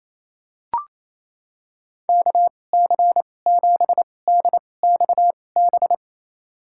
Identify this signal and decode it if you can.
Morse code, heard as a keyed tone, about 28 wpm, spaced as for 20 wpm, 700 Hz. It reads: KC7DXB
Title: USA Call Signs - 6 Characters - 2x Character Spacing - 28wpm